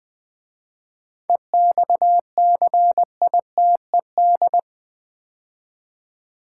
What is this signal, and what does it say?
Morse code, heard as a keyed tone, about 20 wpm, 700 Hz. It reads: EXCITED
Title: 7 Letter Words - First 3 Letters Given - 20wpm